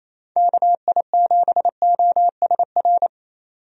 Morse code, heard as a keyed tone, about 28 wpm, 700 Hz. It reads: KI7OSR
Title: USA Call Signs - 6 Characters - 28wpm